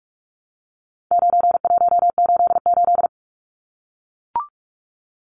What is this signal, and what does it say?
Morse code, heard as a keyed tone, about 44 wpm, 700 Hz. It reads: 9188
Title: Numbers - 4-Digit 45wpm